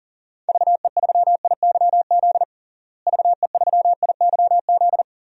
Morse code, heard as a keyed tone, about 40 wpm, 700 Hz. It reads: VE3IYZ VE3IYZ
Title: DXCC-100 Call Signs - 6 Characters - Repeated 3x - 40wpm